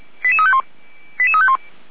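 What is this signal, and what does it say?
Five-tone selective call (ZVEI), heard as quick runs of five tones, tones of about 70 ms. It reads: 88351
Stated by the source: ZVEI five-tone selective call 88349, received by SDR.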